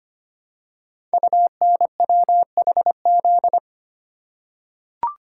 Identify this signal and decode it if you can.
Morse code, heard as a keyed tone, about 25 wpm, 700 Hz. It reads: UNWHZ
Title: Code Groups - 5 Letters - 25wpm